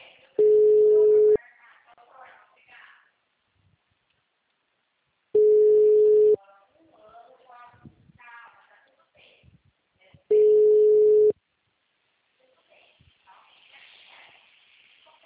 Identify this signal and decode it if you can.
Ringback tone (Europe, 425 Hz), with speech.